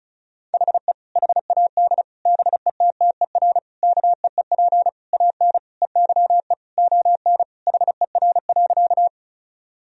Morse code, heard as a keyed tone, about 36 wpm, 700 Hz. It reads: HE HAD BETTER KEEP AN EYE ON HER.
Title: Sentences from Top 1000 Words - Review Part 1 - 35wpm